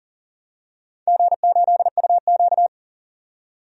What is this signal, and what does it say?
Morse code, heard as a keyed tone, about 43 wpm, 700 Hz. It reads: G8UQ
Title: DXCC-100 Call Signs - 4 Characters - 40wpm